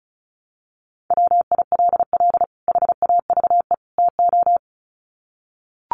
Morse code, heard as a keyed tone, about 34 wpm, 700 Hz. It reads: WILL HAVE TO
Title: Sets of 3 Words 35wpm